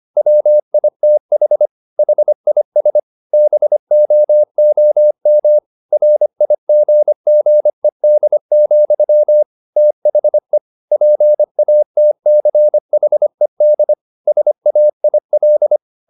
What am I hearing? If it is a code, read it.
Morse code, heard as a keyed tone, about 25 wpm, 600 Hz. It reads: WITH HIS BOOM RIGGED, THE PATCHED SAIL